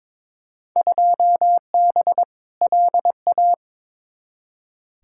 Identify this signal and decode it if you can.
Morse code, heard as a keyed tone, about 22 wpm, 700 Hz. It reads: 2B LA
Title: Field Day Exchanges Encore - 22wpm